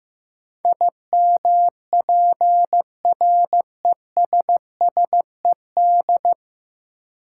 Morse code, heard as a keyed tone, about 15 wpm, 700 Hz. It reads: IMPRESSED
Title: Sentences from Top 1600 Words - New Word Drill Alt - 15wpm